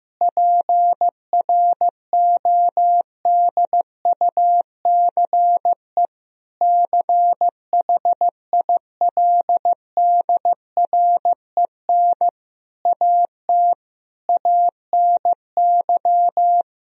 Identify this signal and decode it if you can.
Morse code, heard as a keyed tone, about 15 wpm, 700 Hz. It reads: PRODUCE CHILDREN AT ANY
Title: Sentences from Top 1000 Words - Review Part 3 - 15wpm